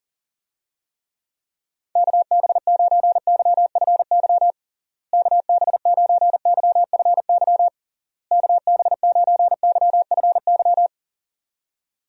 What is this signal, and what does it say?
Morse code, heard as a keyed tone, about 40 wpm, 700 Hz. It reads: KB9YFY KB9YFY KB9YFY
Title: USA Call Signs - 6 Characters - Repeated 3x - 40wpm